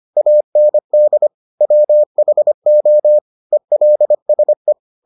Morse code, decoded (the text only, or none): AND WHO ELSE